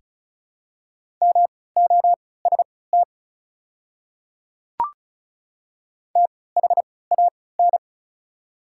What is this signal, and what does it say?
Morse code, heard as a keyed tone, about 36 wpm, spaced as for 20 wpm, 690 Hz. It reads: MOST THAN
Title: 3x Character Spacing - Top 100 Words - 35wpm